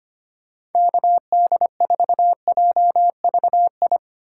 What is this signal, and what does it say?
Morse code, heard as a keyed tone, about 25 wpm, 700 Hz. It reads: KD4JVI
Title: USA Call Signs - 6 Characters - 3x Character Spacing - 25wpm